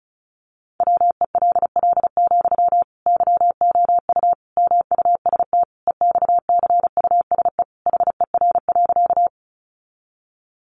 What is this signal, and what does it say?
Morse code, heard as a keyed tone, about 34 wpm, 700 Hz. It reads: WELL, YOU MUST EXCUSE HER.
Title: Sentences from Top 700 Words 35wpm